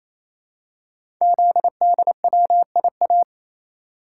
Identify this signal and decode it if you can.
Morse code, heard as a keyed tone, about 27 wpm, 700 Hz. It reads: ZDWIA
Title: Code Groups - 5 Letters - 28wpm